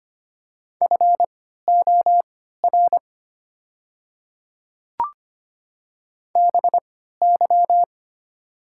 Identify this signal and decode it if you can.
Morse code, heard as a keyed tone, about 25 wpm, spaced as for 14 wpm, 690 Hz. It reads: FOR BY
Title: 3x Character Spacing - Top 100 Words - 25wpm